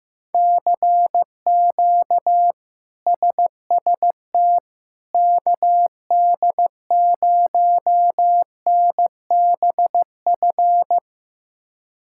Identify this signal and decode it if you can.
Morse code, heard as a keyed tone, about 15 wpm, 700 Hz. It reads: CQ SST KD0NBF
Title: K1USN SST QSO Practice - 15wpm